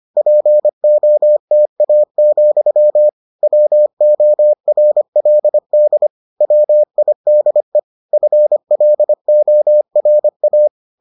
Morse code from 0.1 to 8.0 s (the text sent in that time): POTA, WORLD WIDE